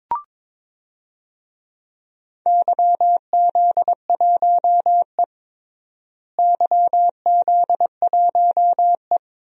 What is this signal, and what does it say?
Morse code, heard as a keyed tone, about 22 wpm, 700 Hz. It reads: YZ1E YZ1E
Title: DXCC-100 Call Signs - 4 Characters - Repeated 2x - 22wpm